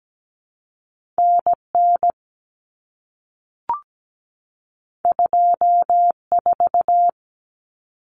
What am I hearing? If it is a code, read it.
Morse code, heard as a keyed tone, about 17 wpm, 700 Hz. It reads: NN 24
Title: Common Cut Numbers - 2-Digit 17wpm